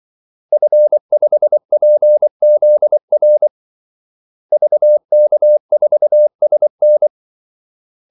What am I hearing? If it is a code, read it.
Morse code, heard as a keyed tone, about 24 wpm, 600 Hz. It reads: F5PZR VK4SN
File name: Callsigns-24wpm